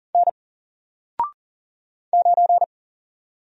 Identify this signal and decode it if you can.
Morse code, heard as a keyed tone, about 39 wpm, 700 Hz. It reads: N 9
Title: Single Letter-Number Alt 40wpm